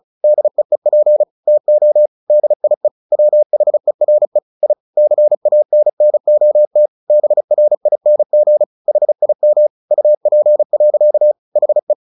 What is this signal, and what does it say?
Morse code, heard as a keyed tone, about 35 wpm, 600 Hz. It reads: DEEP TO DIE WHERE I CANNOT BRING HIM UP. HE
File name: the old man and the sea 2 35wpm